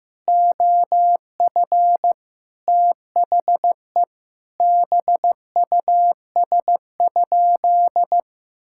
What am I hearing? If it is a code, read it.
Morse code, heard as a keyed tone, about 15 wpm, 700 Hz. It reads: OF THE BUS?
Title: Sentences from Top 1000 Words - Review Part 6 - 15wpm